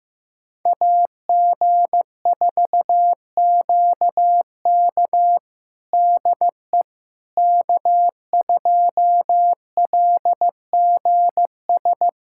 Morse code, heard as a keyed tone, about 16 wpm, 700 Hz. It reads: AG4QK DE K2LGS
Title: USA Callsign DE Callsign - 5 Characters - 15wpm